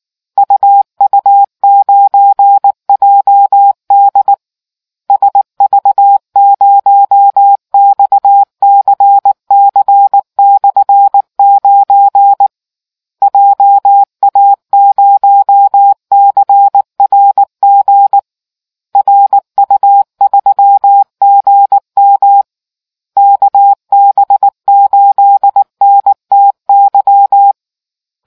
Morse code, heard as a keyed tone, about 19 wpm, 800 Hz. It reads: UU9JD SV0XCC/9 JA0CRG RU3GM KB8NTY